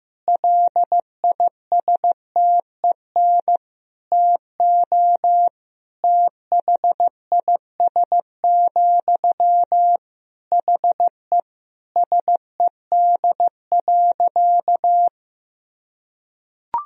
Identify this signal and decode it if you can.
Morse code, heard as a keyed tone, about 15 wpm, 700 Hz. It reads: LISTEN TO THIS, HE SED.
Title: Sentences from Top 400 Words and Abbreviations 15wpm